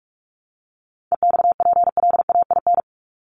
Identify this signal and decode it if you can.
Morse code, heard as a keyed tone, about 44 wpm, 700 Hz. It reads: EXPLAIN
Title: Top 1001-1250 Words 45wpm